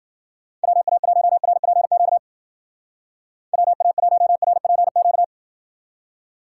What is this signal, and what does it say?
Morse code, heard as a keyed tone, about 60 wpm, 700 Hz. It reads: WA1RPQ WA1RPQ